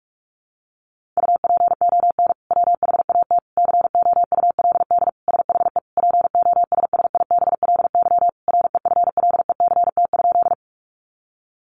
Morse code, heard as a keyed tone, about 44 wpm, 700 Hz. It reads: UPON WHAT COULD SHE POSSIBLY REFLECT?
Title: Sentences from Top 1500 Words 45wpm